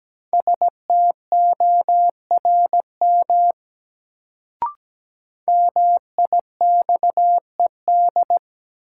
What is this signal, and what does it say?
Morse code, heard as a keyed tone, about 17 wpm, 700 Hz. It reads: STORM MIXED
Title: Sentences from Top 1500 Words - New Word Drill - 17wpm